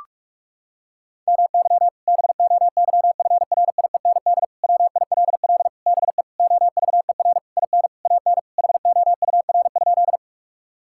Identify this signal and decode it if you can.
Morse code, heard as a keyed tone, about 41 wpm, 700 Hz. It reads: MY BOYFRIEND WILL BE OVER IN AN HOUR?
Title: Sentences from Top 1000 Words - Review Part 5 - 45wpm